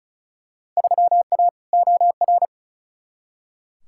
Morse code, heard as a keyed tone, about 36 wpm, 700 Hz. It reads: 3A OR
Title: Field Day Exchanges - 35wpm